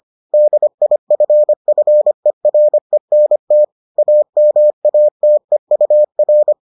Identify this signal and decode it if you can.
Morse code, heard as a keyed tone, about 25 wpm, 600 Hz. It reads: DIFFERENT AMATEUR